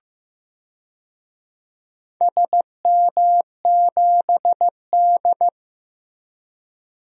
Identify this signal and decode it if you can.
Morse code, heard as a keyed tone, about 15 wpm, 700 Hz. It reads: SM7D
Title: DXCC-100 Call Signs - 4 Characters - Repeated 2x - 15wpm